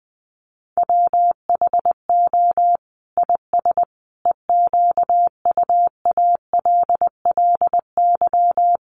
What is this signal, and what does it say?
Morse code, heard as a keyed tone, about 20 wpm, 700 Hz. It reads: WHO IS EQUALLY
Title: Sentences from Top 1400 Words 20wpm